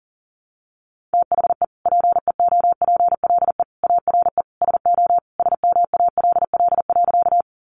Morse code, heard as a keyed tone, about 40 wpm, 700 Hz. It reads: THE PEOPLE ARE SO SMALL.
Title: Additional Easy Phrases 40wpm